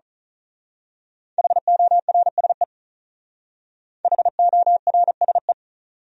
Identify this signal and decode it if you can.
Morse code, heard as a keyed tone, about 39 wpm, 700 Hz. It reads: HORSE HORSE
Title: Speed-Racing Top 300 Words 35wpm